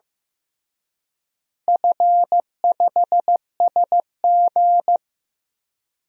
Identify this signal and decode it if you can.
Morse code, heard as a keyed tone, about 15 wpm, 700 Hz. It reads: F5SG
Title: DXCC-100 Call Signs - 4 Characters - Repeated 3x - 15wpm